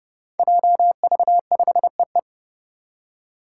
Morse code, heard as a keyed tone, about 30 wpm, 700 Hz. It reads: JV5EE
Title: Five Challenging Letter-Number Combinations 30wpm